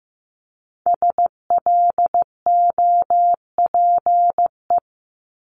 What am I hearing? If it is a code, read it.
Morse code, heard as a keyed tone, about 15 wpm, 700 Hz. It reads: SLOPE
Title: Multi-letter Phonemes 15wpm